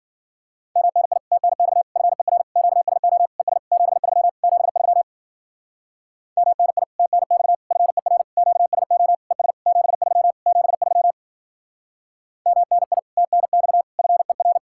Morse code, heard as a keyed tone, about 50 wpm, 700 Hz. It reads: MNI TNX FER QSO ES 73 73 MNI TNX FER QSO ES 73 73 MNI TNX FER
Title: Speed-Racing Common QSO Phrases Encore - 40wpm